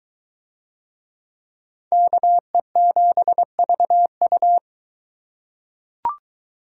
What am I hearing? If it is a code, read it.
Morse code, heard as a keyed tone, about 21 wpm, 700 Hz. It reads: KE7VU